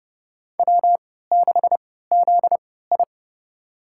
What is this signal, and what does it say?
Morse code, heard as a keyed tone, about 30 wpm, spaced as for 17 wpm, 700 Hz. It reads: W6ZI
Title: USA Call Signs - 4 Characters - 3x Character Spacing - 30wpm